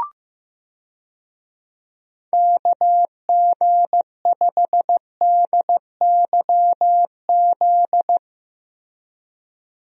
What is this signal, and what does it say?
Morse code, heard as a keyed tone, about 16 wpm, 700 Hz. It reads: KG5DYZ
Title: USA Call Signs - 6 Characters - Repeated 2x - 15wpm